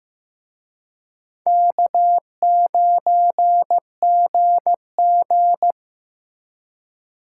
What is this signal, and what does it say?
Morse code, heard as a keyed tone, about 18 wpm, 700 Hz. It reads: K9GG
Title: Speed Racing - USA 1x2 Call Signs 15wpm